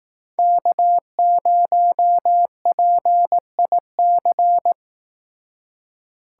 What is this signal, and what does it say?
Morse code, heard as a keyed tone, about 18 wpm, 700 Hz. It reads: K0PIC
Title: Speed Racing - USA Call Signs 15wpm